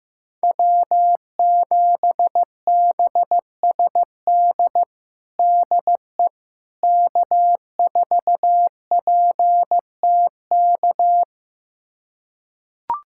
Morse code, heard as a keyed tone, about 15 wpm, 700 Hz. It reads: W7BSD DE K4PTK
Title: USA Callsign DE Callsign - 5 Characters - 15wpm